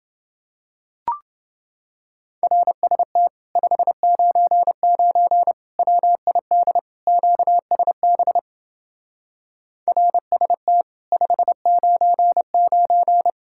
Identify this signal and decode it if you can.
Morse code, heard as a keyed tone, about 28 wpm, 700 Hz. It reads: RST 599 WID QSB RST 599